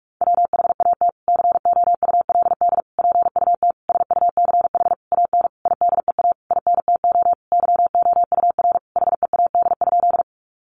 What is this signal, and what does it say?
Morse code, heard as a keyed tone, about 44 wpm, 700 Hz. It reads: WHAT COULD PUT SUCH AN IDEA INTO YOUR HEAD?
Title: Sentences from Top 600 Words 45wpm